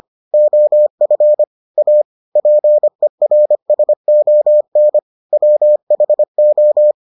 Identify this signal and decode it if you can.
Morse code, heard as a keyed tone, about 25 wpm, 600 Hz. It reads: OF A PERSON WHO